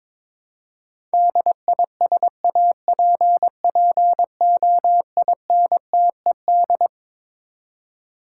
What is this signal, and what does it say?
Morse code, heard as a keyed tone, about 22 wpm, 700 Hz. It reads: DISAPPOINTED
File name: Twelve Letter Words - 22wpm